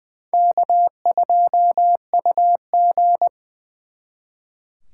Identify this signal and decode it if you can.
Morse code, heard as a keyed tone, about 24 wpm, 700 Hz. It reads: K2UG
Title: Speed Racing - USA 1x2 Call Signs 20wpm